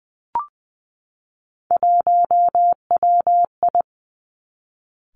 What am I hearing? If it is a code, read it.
Morse code, heard as a keyed tone, about 20 wpm, 700 Hz. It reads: 1WI
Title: Three Challenging Letter-Number Combinations 20wpm